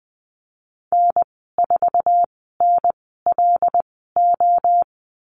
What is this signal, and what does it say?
Morse code, heard as a keyed tone, about 20 wpm, spaced as for 14 wpm, 700 Hz. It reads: N4NLO
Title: 076 Lesson 13g - Letter L - Call Signs 20wpm